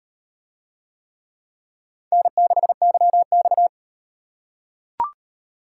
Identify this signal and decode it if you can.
Morse code, heard as a keyed tone, about 38 wpm, 700 Hz. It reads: N6YX